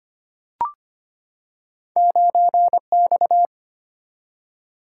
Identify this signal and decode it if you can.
Morse code, heard as a keyed tone, about 25 wpm, 700 Hz. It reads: 9X
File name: Two Letter-Number Combinations - 25wpm